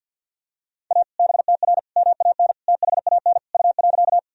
Morse code, heard as a keyed tone, about 46 wpm, 700 Hz. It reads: A BTR MAN THAN U.
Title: Sentences from Top 400 Words and Abbreviations 50wpm